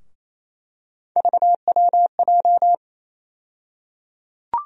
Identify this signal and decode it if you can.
Morse code, heard as a keyed tone, about 28 wpm, 700 Hz. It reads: VWJ
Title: USA Call Sign Postfixes - 3 Letters 28wpm